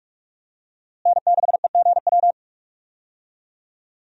Morse code, heard as a keyed tone, about 46 wpm, 700 Hz. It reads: N6EGW
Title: USA Call Signs - 5 Characters - Repeated 3x - 45wpm